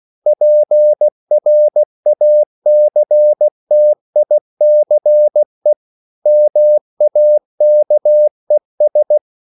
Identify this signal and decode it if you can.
Morse code, heard as a keyed tone, about 16 wpm, 600 Hz. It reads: PRACTICE MAKES
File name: CW/Practice